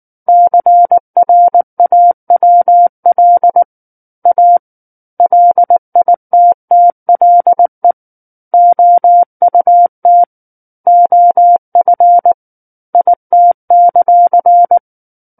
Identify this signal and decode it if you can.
Morse code, heard as a keyed tone, about 19 wpm, 700 Hz. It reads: CRAWL A LITTLE OUT OF IT;
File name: Book One / 10-Chapter 10 - 19 WPM